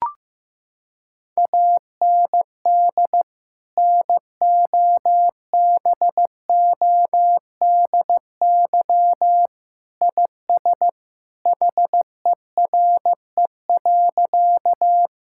Morse code, heard as a keyed tone, about 16 wpm, 700 Hz. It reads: AND NOBODY IS HERE.
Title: Sentences from Top 1600 Words 15wpm